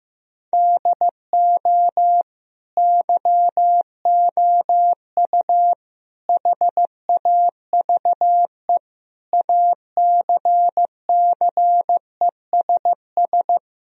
Morse code, heard as a keyed tone, about 15 wpm, 700 Hz. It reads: DO YOU HAVE ACCESS
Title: Sentences from Top 1000 Words - Review Part 4 - 15wpm